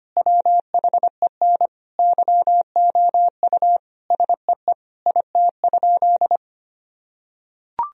Morse code, heard as a keyed tone, about 25 wpm, 700 Hz. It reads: WHEN YOU SEE IT?
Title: Sentences from Top 1000 Words - Review Part 4 - 25wpm